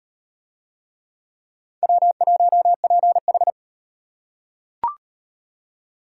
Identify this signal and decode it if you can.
Morse code, heard as a keyed tone, about 38 wpm, 700 Hz. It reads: W1PH